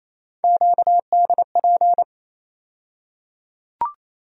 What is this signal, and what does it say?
Morse code, heard as a keyed tone, about 28 wpm, 700 Hz. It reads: QDP
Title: USA Call Sign Postfixes - 3 Letters 28wpm